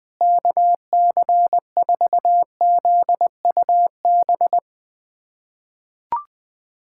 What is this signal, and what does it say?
Morse code, heard as a keyed tone, about 20 wpm, 700 Hz. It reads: KC4ZUB
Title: USA Callsign DE Callsign - 6 Characters - 20wpm